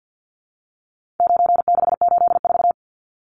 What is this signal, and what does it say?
Morse code, heard as a keyed tone, about 50 wpm, 700 Hz. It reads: 9684
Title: Numbers - 4-Digit 50wpm